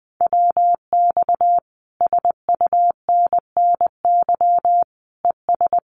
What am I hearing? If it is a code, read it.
Morse code, heard as a keyed tone, about 20 wpm, 700 Hz. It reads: WX SUNNY ES
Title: Common QSO Phrases Encore 20wpm